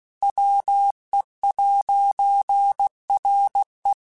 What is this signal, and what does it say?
Morse code, heard as a keyed tone, about 16 wpm, 800 Hz. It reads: WE'RE
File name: Technical Subjects / Technical Subjects-mp3s / AND Gate @15